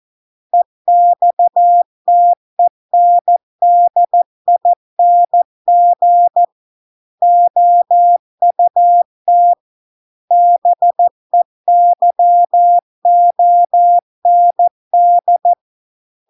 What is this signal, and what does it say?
Morse code, heard as a keyed tone, about 14 wpm, 700 Hz. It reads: EXTENDING OUT BEYOND